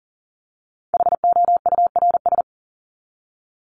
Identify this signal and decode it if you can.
Morse code, heard as a keyed tone, about 40 wpm, 700 Hz. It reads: HOURS
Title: ES:S Suffix - Top 100 Words with Base Words 40wpm